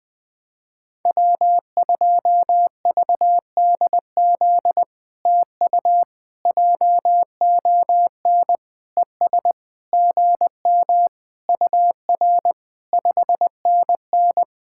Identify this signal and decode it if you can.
Morse code, heard as a keyed tone, about 20 wpm, 700 Hz. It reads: W2VDZ TU JON ES GM UR 5NN
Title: POTA QSO Practice - 20wpm